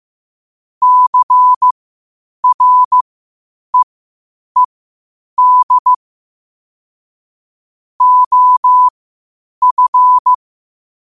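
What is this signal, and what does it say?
Morse code, heard as a keyed tone, about 15 wpm, spaced as for 8 wpm, 1000 Hz. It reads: CREED OF
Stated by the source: Morse practice at 8 WPM from Morse Resource.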